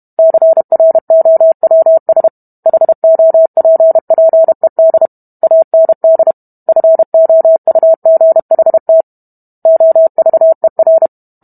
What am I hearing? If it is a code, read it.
Morse code, heard as a keyed tone, about 32 wpm, 650 Hz. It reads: CROWS HOPPED AND FOUGHT OVER